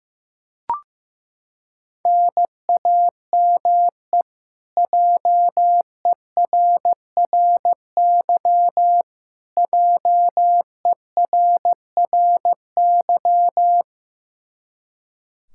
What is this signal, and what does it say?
Morse code, heard as a keyed tone, about 15 wpm, 700 Hz. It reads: NAME JERRY JERRY
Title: Practice QSO - Exchange 1 - Beginning - 15wpm